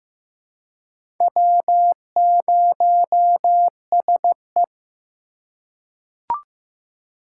Morse code, heard as a keyed tone, about 15 wpm, 700 Hz. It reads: W0SE
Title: USA Call Signs - 4 Characters - 4x Character Spacing - 15wpm